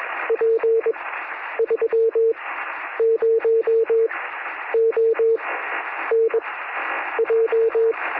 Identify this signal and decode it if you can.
Morse code, heard as a keyed tone, about 21 wpm, spaced as for 10 wpm, 450 Hz, with noise. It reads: P30ON J
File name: sm7lcb / 1957z oh5lk 23cm